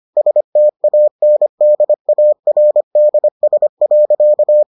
Morse code, heard as a keyed tone, about 25 wpm, 600 Hz. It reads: STANDARDS.